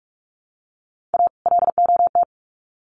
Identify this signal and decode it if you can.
Morse code, heard as a keyed tone, about 44 wpm, 700 Hz. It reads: A LOT
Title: Sets of 2 Words 45wpm